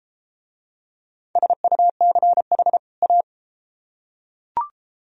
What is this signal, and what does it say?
Morse code, heard as a keyed tone, about 30 wpm, 700 Hz. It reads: SUCH A